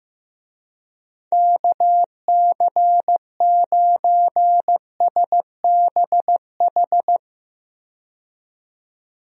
Morse code, heard as a keyed tone, about 15 wpm, 700 Hz. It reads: KC9SBH
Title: USA Call Signs - 6 Characters - 15wpm